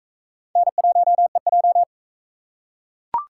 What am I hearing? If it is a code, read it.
Morse code, heard as a keyed tone, about 44 wpm, 700 Hz. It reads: N1EJ